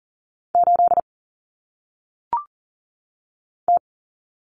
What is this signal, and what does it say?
Morse code, heard as a keyed tone, about 39 wpm, 700 Hz. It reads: 8 T
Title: Common Cut Numbers - 1-Digit 40wpm